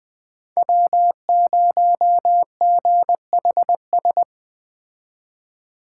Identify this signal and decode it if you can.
Morse code, heard as a keyed tone, about 20 wpm, 700 Hz. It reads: W0GHS